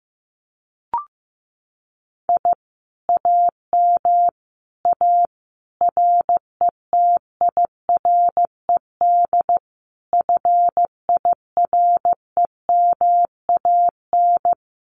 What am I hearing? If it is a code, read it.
Morse code, heard as a keyed tone, about 16 wpm, 700 Hz. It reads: I AM A RETIRED FIREMAN
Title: Common QSO Phrases Encore 15wpm